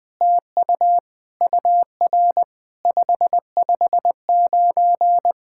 Morse code, heard as a keyed tone, about 21 wpm, 700 Hz. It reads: TU UR 559
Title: SOTA QSO Practice 20wpm